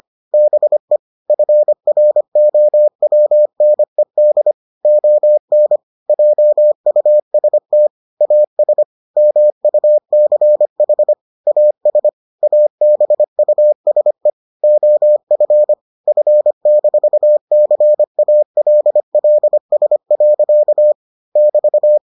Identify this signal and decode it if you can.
Morse code, heard as a keyed tone, about 25 wpm, 600 Hz. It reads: BE FROWNED ON JUST AS MUCH AS ABUSE OF F-CALLS. <BT>